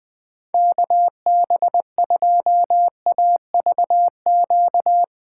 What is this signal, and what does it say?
Morse code, heard as a keyed tone, about 20 wpm, 700 Hz. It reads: KB2AVQ
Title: USA Call Signs - 6 Characters - 6x Character Spacing - 20wpm